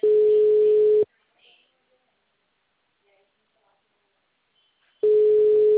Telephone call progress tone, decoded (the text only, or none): ringback tone